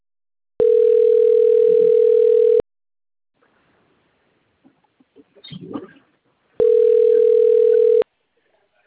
Ringback tone (North America), 440 + 480 Hz.